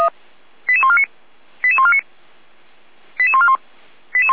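Five-tone selective call (ZVEI), heard as quick runs of five tones, tones of about 70 ms. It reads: 88149 88149 88141
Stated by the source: ZVEI five-tone call 88241 received by SDR.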